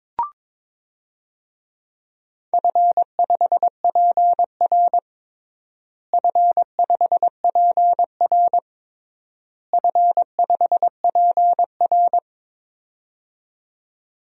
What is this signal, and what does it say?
Morse code, heard as a keyed tone, about 22 wpm, 700 Hz. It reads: F5PR F5PR F5PR
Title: DXCC-100 Call Signs - 4 Characters - Repeated 3x - 22wpm